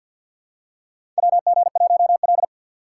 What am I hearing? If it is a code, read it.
Morse code, heard as a keyed tone, about 50 wpm, 700 Hz. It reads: WG1L
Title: USA Call Signs - 4 Characters - 3x Character Spacing - 50wpm